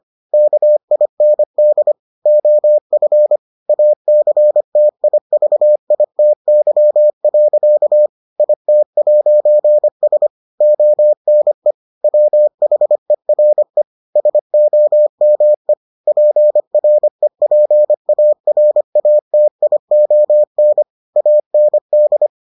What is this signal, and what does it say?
Morse code, heard as a keyed tone, about 25 wpm, 600 Hz. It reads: KIND OF ACTIVITY. IT'S ONE WHERE SOME PREPARATION AND